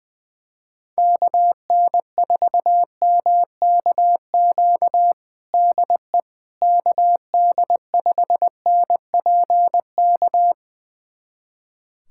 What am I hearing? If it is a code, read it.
Morse code, heard as a keyed tone, about 20 wpm, 700 Hz. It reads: KN4MKQ DE KD5NPK
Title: USA Callsign DE Callsign - 6 Characters - 20wpm